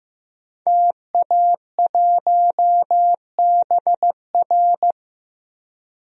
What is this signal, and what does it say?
Morse code, heard as a keyed tone, about 15 wpm, 700 Hz. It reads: TA1BR
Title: DXCC-100 Call Signs - 5 Characters - Repeated 2x - 15wpm